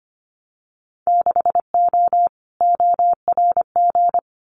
Morse code, heard as a keyed tone, about 25 wpm, 700 Hz. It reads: 6O ORG
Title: Winter Field Day Exchanges Encore 25wpm